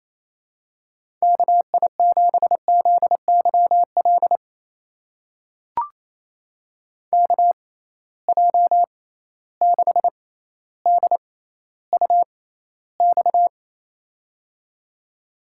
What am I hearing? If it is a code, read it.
Morse code, heard as a keyed tone, about 28 wpm, spaced as for 14 wpm, 700 Hz. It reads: KI7ZYL KJ6DUX
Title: USA Call Signs - 6x Character Spacing - 28wpm